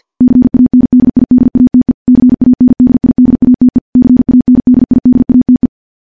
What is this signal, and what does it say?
Morse code, heard as a keyed tone, about 50 wpm, 250 Hz. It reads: CANDIDATE CANDIDATE CANDIDATE
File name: morse / morse-training-50WPM